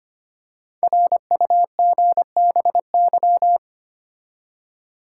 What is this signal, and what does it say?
Morse code, heard as a keyed tone, about 25 wpm, 700 Hz. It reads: RUGBY
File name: 5 Letter Words - First 3 Letters Given - 25wpm